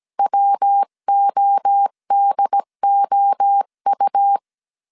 Morse code, heard as a keyed tone, about 17 wpm, 800 Hz. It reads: WODOU
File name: CW / DET / 1 3 1500C S A 16wpm 2141